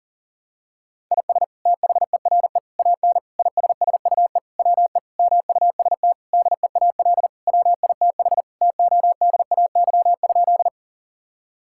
Morse code, heard as a keyed tone, about 40 wpm, 700 Hz. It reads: IS THERE AN ISSUE WE MUST DEAL WITH TODAY?
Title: Sentences from Top 1000 Words - Review Part 6 - 40wpm